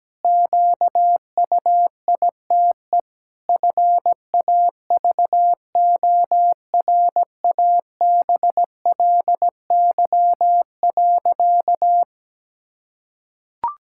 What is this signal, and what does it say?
Morse code, heard as a keyed tone, about 17 wpm, 700 Hz. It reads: QUITE FAVORABLY.
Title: Pangram Challenge v2 - 17wpm